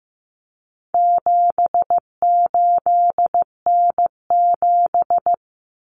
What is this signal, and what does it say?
Morse code, heard as a keyed tone, about 15 wpm, 700 Hz. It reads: 78N7
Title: Common Cut Numbers - 4-Digit 15wpm